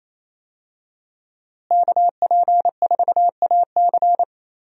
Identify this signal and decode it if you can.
Morse code, heard as a keyed tone, about 28 wpm, 700 Hz. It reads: KP4AC
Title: USA Call Signs - 5 Characters - Repeated 3x - 28wpm